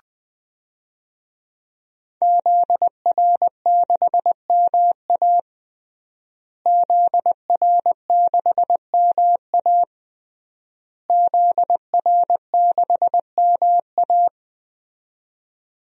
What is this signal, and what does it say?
Morse code, heard as a keyed tone, about 20 wpm, 700 Hz. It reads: ZR6MA ZR6MA ZR6MA
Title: DXCC-100 Call Signs - 5 Characters - Repeated 3x - 20wpm